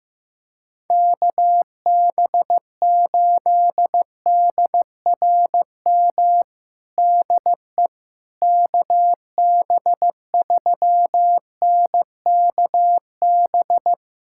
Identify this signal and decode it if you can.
Morse code, heard as a keyed tone, about 15 wpm, 700 Hz. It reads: KB8DRM DE KB3NKB
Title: USA Callsign DE Callsign - 6 Characters - 15wpm